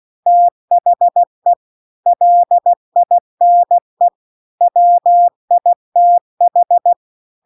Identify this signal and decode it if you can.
Morse code, heard as a keyed tone, about 16 wpm, 700 Hz. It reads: THE LINE WITH
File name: CW-The-Big-Twoheaded-River-2-16WPM